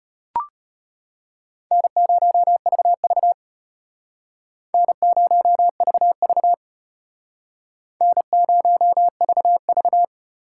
Morse code, heard as a keyed tone, about 34 wpm, 700 Hz. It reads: N0VV N0VV N0VV